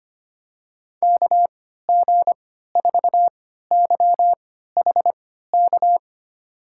Morse code, heard as a keyed tone, about 25 wpm, spaced as for 14 wpm, 700 Hz. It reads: KG4YHK
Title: USA Call Signs - 6 Characters - 3x Character Spacing - 25wpm